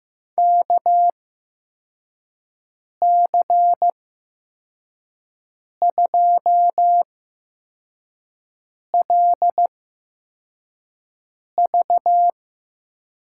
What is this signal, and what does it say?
Morse code, heard as a keyed tone, about 15 wpm, spaced as for 4 wpm, 700 Hz. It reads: KC2LV
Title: USA Call Signs - 6 Characters - 8x Character Spacing - 15wpm